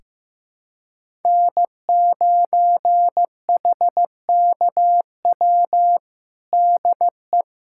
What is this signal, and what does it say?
Morse code, heard as a keyed tone, about 15 wpm, 700 Hz. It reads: N9HKW DE
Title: USA Callsign DE Callsign - 5 Characters - 15wpm